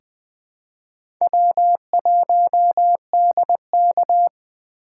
Morse code, heard as a keyed tone, about 20 wpm, 700 Hz. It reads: W1DK
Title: USA Call Signs - 8x Character Spacing - 20wpm